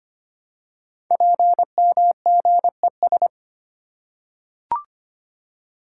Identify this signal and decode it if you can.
Morse code, heard as a keyed tone, about 25 wpm, 700 Hz. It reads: PMGES
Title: Code Groups - 5 Letters - 25wpm